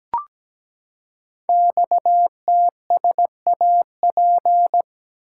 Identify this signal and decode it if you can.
Morse code, heard as a keyed tone, about 17 wpm, 700 Hz. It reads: XTSAP
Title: Code Groups - 5 Letters - 17wpm